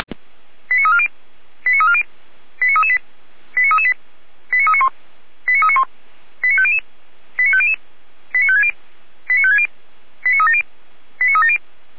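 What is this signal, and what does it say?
Five-tone selective call (ZVEI), heard as quick runs of five tones, tones of about 80 ms. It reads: 89340 89340 89338 89338 89391 89391 89500 89500 89560 89560 89370 89370
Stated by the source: ZVEI five-tone call received on SDR.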